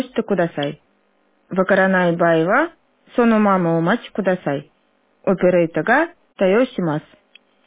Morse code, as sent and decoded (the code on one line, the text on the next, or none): none